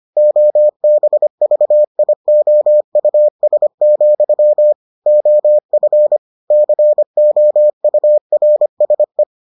Morse code, heard as a keyed tone, about 25 wpm, 600 Hz. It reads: OBVIOUS, OF COURSE